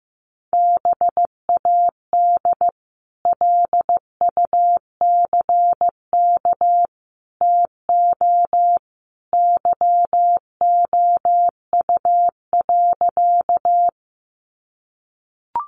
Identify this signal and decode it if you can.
Morse code, heard as a keyed tone, about 15 wpm, 700 Hz. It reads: BAD LUCK TO YOU.
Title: Sentences from Top 1500 Words 15wpm